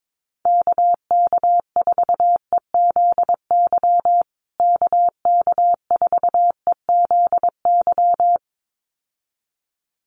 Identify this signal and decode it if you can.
Morse code, heard as a keyed tone, about 22 wpm, 700 Hz. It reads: KK4EZY KK4EZY
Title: USA Call Signs Repeated 2x 22wpm